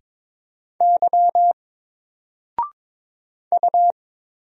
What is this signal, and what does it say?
Morse code, heard as a keyed tone, about 22 wpm, 700 Hz. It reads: Y U
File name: Single-Letter-Number-22wpm